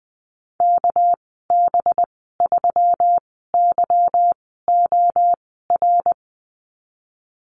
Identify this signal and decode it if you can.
Morse code, heard as a keyed tone, about 20 wpm, spaced as for 15 wpm, 700 Hz. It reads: KB3YOR
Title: 231 Lesson 37g - Letter x - Call Signs 20wpm